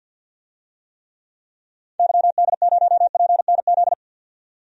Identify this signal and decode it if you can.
Morse code, heard as a keyed tone, about 50 wpm, 700 Hz. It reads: YD0PNZ
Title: DXCC-100 Call Signs - 6 Characters - 50wpm